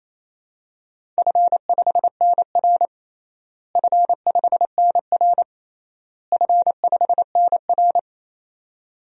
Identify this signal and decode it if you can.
Morse code, heard as a keyed tone, about 28 wpm, 700 Hz. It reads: F5NR F5NR F5NR
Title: DXCC-100 Call Signs - 4 Characters - Repeated 3x - 28wpm